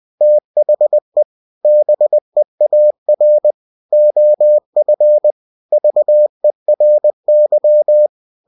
Morse code, heard as a keyed tone, about 20 wpm, 600 Hz. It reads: THE BEAR OF VERY